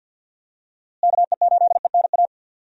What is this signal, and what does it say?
Morse code, heard as a keyed tone, about 50 wpm, 700 Hz. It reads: KE8ENA
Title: USA Call Signs - 6 Characters - Repeated 3x - 50wpm